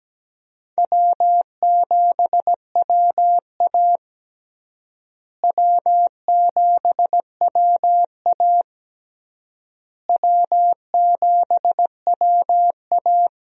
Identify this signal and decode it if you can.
Morse code, heard as a keyed tone, about 17 wpm, 700 Hz. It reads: W7WA W7WA W7WA
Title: USA Call Signs - 4 Characters - Repeated 3x - 17wpm